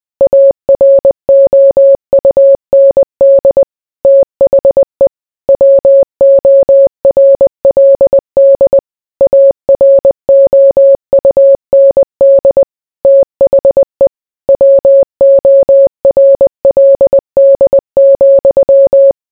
Morse code, heard as a keyed tone, about 20 wpm, 550 Hz. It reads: AROUND THE WORLD AROUND THE WORLD,